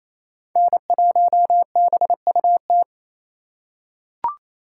Morse code, heard as a keyed tone, about 28 wpm, 700 Hz. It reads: N1BUT
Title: USA Call Signs - 5 Characters - 4x Character Spacing - 28wpm